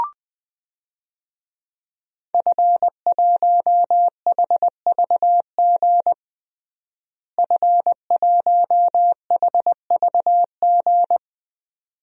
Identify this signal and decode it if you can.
Morse code, heard as a keyed tone, about 21 wpm, 700 Hz. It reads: F1HVG F1HVG
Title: DXCC-100 Call Signs - 5 Characters - Repeated 2x - 20wpm